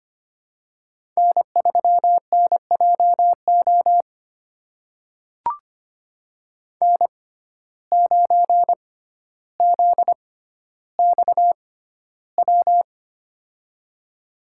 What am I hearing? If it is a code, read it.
Morse code, heard as a keyed tone, about 25 wpm, spaced as for 13 wpm, 700 Hz. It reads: N3NJO N9ZXW
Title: USA Call Signs - 5 Characters - 6x Character Spacing - 25wpm